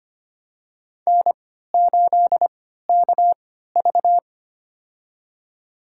Morse code, heard as a keyed tone, about 25 wpm, spaced as for 14 wpm, 700 Hz. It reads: N8KV
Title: USA Call Signs - 4 Characters - 3x Character Spacing - 25wpm